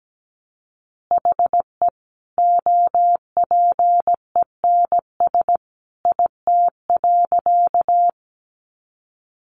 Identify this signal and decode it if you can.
Morse code, heard as a keyed tone, about 17 wpm, 700 Hz. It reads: HE OPENS IT.
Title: Additional Easy Phrases 17wpm